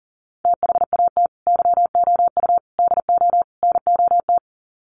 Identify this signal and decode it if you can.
Morse code, heard as a keyed tone, about 40 wpm, 700 Hz. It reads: THAT YOU DO NOT
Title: Sets of 4 Words 40wpm